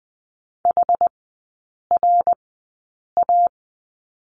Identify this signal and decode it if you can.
Morse code, heard as a keyed tone, about 20 wpm, spaced as for 8 wpm, 700 Hz. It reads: HRA
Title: 212 Lesson 34e - Number 8 - Three Characters 20wpm